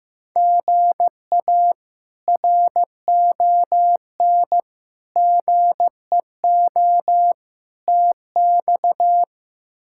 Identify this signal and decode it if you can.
Morse code, heard as a keyed tone, about 15 wpm, 700 Hz. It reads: GA RON GEO TX
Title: K1USN SST QSO Practice - 15wpm